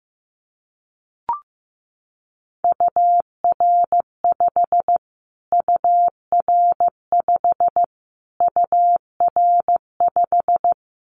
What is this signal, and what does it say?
Morse code, heard as a keyed tone, about 15 wpm, 700 Hz. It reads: UR5 UR5 UR5
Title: Callsign Prefixes for Top 25 DXCC - Repeated 3x - 15wpm